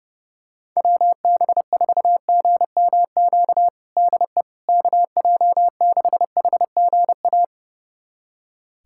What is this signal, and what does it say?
Morse code, heard as a keyed tone, about 30 wpm, 700 Hz. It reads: WB4GMQ DE KJ6HGA
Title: USA Callsign DE Callsign - 6 Characters - 30wpm